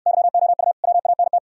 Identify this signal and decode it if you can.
Morse code, heard as a keyed tone, about 66 wpm, 700 Hz. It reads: YOU WANT